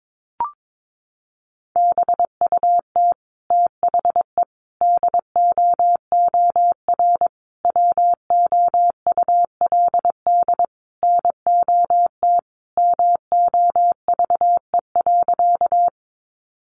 Morse code, heard as a keyed tone, about 22 wpm, 700 Hz. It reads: BUT THE DOOR WOULD NOT MOVE.
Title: Sentences from Top 600 Words 22wpm